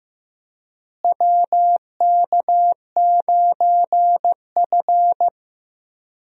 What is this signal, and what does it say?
Morse code, heard as a keyed tone, about 16 wpm, 700 Hz. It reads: WK9F
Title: USA Call Signs - 4 Characters - Repeated 3x - 15wpm